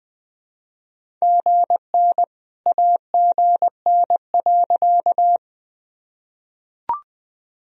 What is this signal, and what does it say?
Morse code, heard as a keyed tone, about 20 wpm, 700 Hz. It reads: GN AGN.
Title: Sentences from Top 300 Words and Abbreviations 20wpm